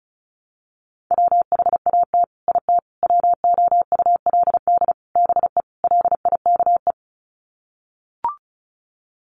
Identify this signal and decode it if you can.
Morse code, heard as a keyed tone, about 34 wpm, 700 Hz. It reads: WHAT IT WOULD BE LIKE
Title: Sets of 5 Words Encore 35wpm